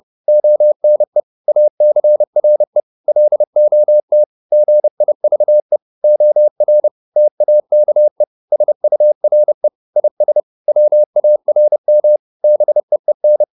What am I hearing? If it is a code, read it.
Morse code, heard as a keyed tone, about 30 wpm, 600 Hz. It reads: ONE ACRE LOT GIVE OR TAKE SURE IS WARM BEEN